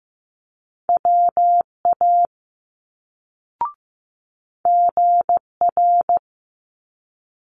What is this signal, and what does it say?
Morse code, heard as a keyed tone, about 15 wpm, 700 Hz. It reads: WA GR
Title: Multi-letter Phonemes Quick Fire 15wpm